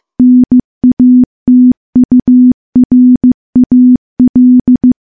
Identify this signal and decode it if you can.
Morse code, heard as a keyed tone, about 15 wpm, 250 Hz. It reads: NATURAL
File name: morse-training-15WPM